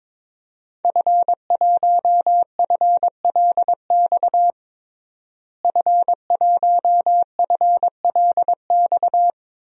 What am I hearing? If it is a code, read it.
Morse code, heard as a keyed tone, about 22 wpm, 700 Hz. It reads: F1FLX F1FLX
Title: DXCC-100 Call Signs - 5 Characters - Repeated 3x - 22wpm